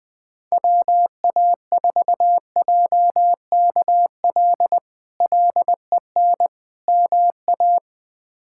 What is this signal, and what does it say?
Morse code, heard as a keyed tone, about 21 wpm, 700 Hz. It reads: WA4JKL LEN MA
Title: K1USN SST QSO Practice - 20wpm